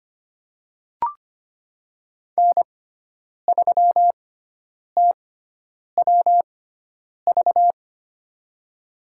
Morse code, heard as a keyed tone, about 25 wpm, spaced as for 9 wpm, 700 Hz. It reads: N3TWV
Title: USA Call Signs - 5 Characters - 6x Character Spacing - 25wpm